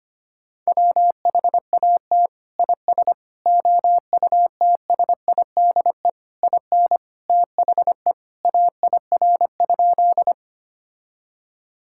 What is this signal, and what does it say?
Morse code, heard as a keyed tone, about 25 wpm, 700 Hz. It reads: WHAT IS OUTSIDE IN THE AIR?
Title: Sentences from Top 1000 Words - Review Part 6 - 25wpm